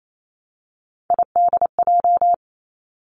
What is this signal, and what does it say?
Morse code, heard as a keyed tone, about 28 wpm, 700 Hz. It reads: IDJ
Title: Three Challenging Letter-Number Combinations 28wpm